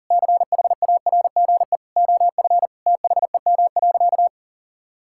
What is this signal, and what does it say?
Morse code, heard as a keyed tone, about 40 wpm, 700 Hz. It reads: CHARGE OF THEM.
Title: Sentences from Top 1000 Words - Review Part 6 - 40wpm